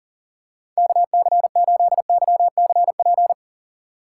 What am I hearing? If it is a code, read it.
Morse code, heard as a keyed tone, about 40 wpm, 690 Hz. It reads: KC8YCP
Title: USA Call Signs - 6 Characters - 2x Character Spacing - 40wpm